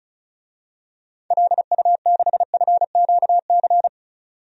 Morse code, heard as a keyed tone, about 36 wpm, 700 Hz. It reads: LU6FQC
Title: DXCC-100 Call Signs - 6 Characters - 35wpm